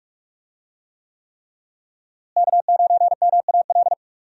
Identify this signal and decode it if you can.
Morse code, heard as a keyed tone, about 42 wpm, 700 Hz. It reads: K9MAL
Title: USA Call Signs - 5 Characters - Repeated 3x - 45wpm